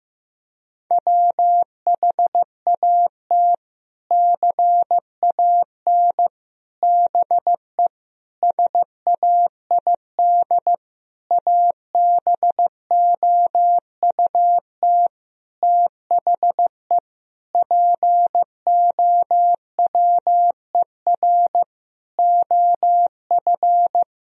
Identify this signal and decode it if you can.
Morse code, heard as a keyed tone, about 15 wpm, 700 Hz. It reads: WHAT CAN BE SAID ABOUT THE POWER OF
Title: Sentences from Top 1000 Words - Review Part 4 - 15wpm